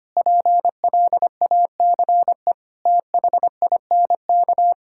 Morse code, heard as a keyed tone, about 25 wpm, 700 Hz. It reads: PLACE THINK
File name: practice / 25wpm / Sentences with 3-Word Phrases - 25wpm